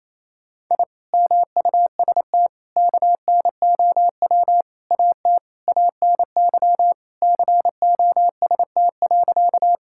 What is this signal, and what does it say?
Morse code, heard as a keyed tone, about 28 wpm, 700 Hz. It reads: I MUST KNOW AT ANY COST.
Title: Sentences from Top 1000 Words - Review Part 5 - 28wpm